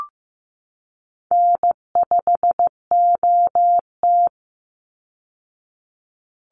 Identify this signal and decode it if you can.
Morse code, heard as a keyed tone, about 15 wpm, 700 Hz. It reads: N5OT
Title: Call Signs Repeated 3x - 15wpm